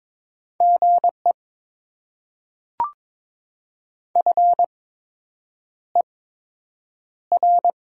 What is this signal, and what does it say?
Morse code, heard as a keyed tone, about 22 wpm, spaced as for 8 wpm, 700 Hz. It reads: GE FER
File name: Top 20 QSO Elements - 8x Character Spacing - 22wpm